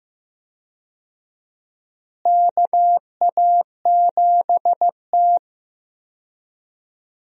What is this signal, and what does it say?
Morse code, heard as a keyed tone, about 15 wpm, 700 Hz. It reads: KA7T
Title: USA Call Signs - 4 Characters - Repeated 2x - 15wpm